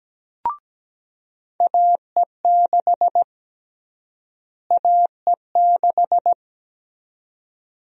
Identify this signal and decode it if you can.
Morse code, heard as a keyed tone, about 17 wpm, 700 Hz. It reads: AE6 AE6
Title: USA Call Sign Prefixes - Repeated 2x - 17wpm